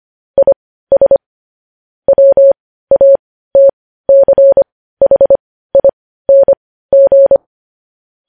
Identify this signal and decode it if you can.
Morse code, heard as a keyed tone, about 25 wpm, spaced as for 15 wpm, 550 Hz. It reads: IS WATCHING
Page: news-headlines-in-morse-code-at-15-wpm